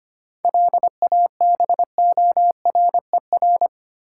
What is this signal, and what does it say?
Morse code, heard as a keyed tone, about 25 wpm, 700 Hz. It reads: LABORER
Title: K1USN SST QSO Practice 25wpm